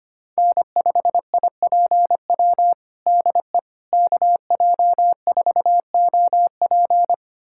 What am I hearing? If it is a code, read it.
Morse code, heard as a keyed tone, about 25 wpm, 700 Hz. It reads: N5IPW DE KJ4OP
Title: USA Callsign DE Callsign - 5 Characters - 25wpm